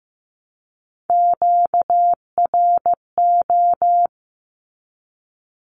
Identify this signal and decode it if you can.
Morse code, heard as a keyed tone, about 15 wpm, 700 Hz. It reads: QRO
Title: Q-Codes Repeated 3x 15wpm